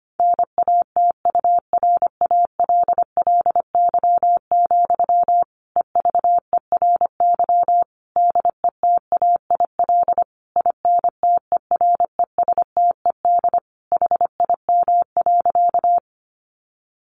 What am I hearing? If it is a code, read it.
Morse code, heard as a keyed tone, about 25 wpm, 700 Hz. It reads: NATURALLY, EVERY DETAIL INTERESTED HIM.
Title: Sentences from Top 1500 Words 25wpm